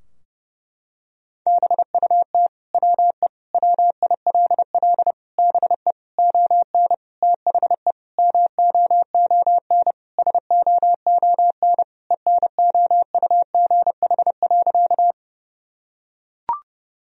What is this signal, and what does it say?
Morse code, heard as a keyed tone, about 32 wpm, 700 Hz. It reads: BUT WE WILL BE ON THE MOON SOON ENOUGH.
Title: Sentences from Top 1000 Words - Review Part 3 - 30wpm